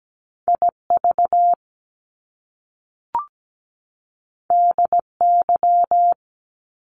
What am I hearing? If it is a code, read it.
Morse code, heard as a keyed tone, about 17 wpm, 700 Hz. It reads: IV DY
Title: Top Two Letter Combinations in Words 17wpm